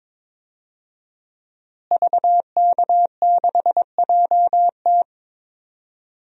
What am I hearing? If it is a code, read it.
Morse code, heard as a keyed tone, about 22 wpm, 700 Hz. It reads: VK6JT
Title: DXCC-100 Call Signs - 5 Characters - Repeated 2x - 22wpm